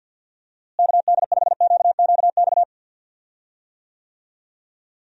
Morse code, heard as a keyed tone, about 49 wpm, 700 Hz. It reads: KD5QQX
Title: USA Call Signs - 6 Characters - Repeated 2x - 50wpm